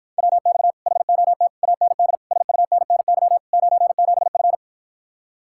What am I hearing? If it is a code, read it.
Morse code, heard as a keyed tone, about 54 wpm, 700 Hz. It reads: WX HOT AND SUNNY 97F